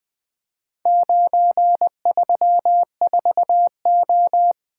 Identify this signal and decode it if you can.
Morse code, heard as a keyed tone, about 20 wpm, 700 Hz. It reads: 934O
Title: Four Letter-Number Combinations - 20wpm